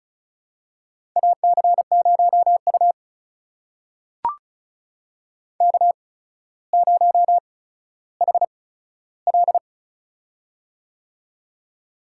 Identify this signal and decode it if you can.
Morse code, heard as a keyed tone, about 36 wpm, spaced as for 15 wpm, 700 Hz. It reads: AC0U K0HL
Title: USA Call Signs - 4 Characters - 8x Character Spacing - 35wpm